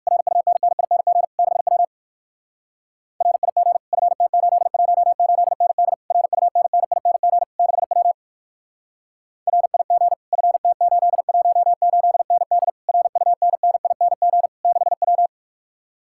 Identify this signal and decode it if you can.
Morse code, heard as a keyed tone, about 68 wpm, spaced as for 59 wpm, 700 Hz. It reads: RUNNING 6W RIG FT818ND RUNNING 6W RIG FT818ND RUNNING 6W